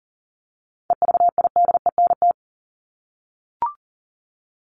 Morse code, heard as a keyed tone, about 37 wpm, 700 Hz. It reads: EVIDENT
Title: LY Suffix - Top 100 Words with Base Words 40wpm